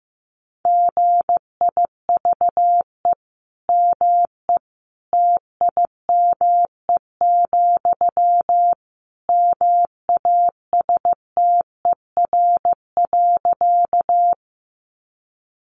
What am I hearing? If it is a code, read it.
Morse code, heard as a keyed tone, about 15 wpm, 700 Hz. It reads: GIVE ME TIME, MASTER.
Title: Sentences from Top 1300 Words 15wpm